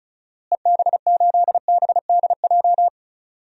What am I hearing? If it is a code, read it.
Morse code, heard as a keyed tone, about 34 wpm, 700 Hz. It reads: EB8BDJ
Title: DXCC-100 Call Signs - 6 Characters - Repeated 3x - 35wpm